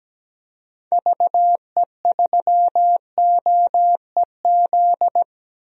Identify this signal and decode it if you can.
Morse code, heard as a keyed tone, about 17 wpm, 700 Hz. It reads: VE3OEZ
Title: DXCC-100 Call Signs - 6 Characters - 17wpm